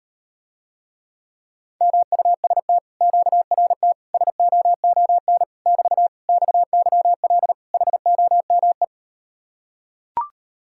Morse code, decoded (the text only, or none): MUST QRT SOON <BT> XYL HOME